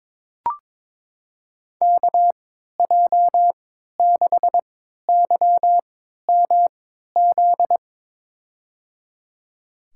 Morse code, heard as a keyed tone, about 21 wpm, spaced as for 12 wpm, 700 Hz. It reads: KJ6YMZ
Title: USA Call Signs - 3x Character Spacing - 22wpm